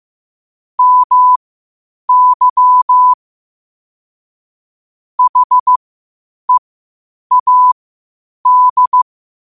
Morse code, heard as a keyed tone, about 15 wpm, spaced as for 8 wpm, 1000 Hz. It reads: MY HEAD
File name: qotdfiles / 8wpm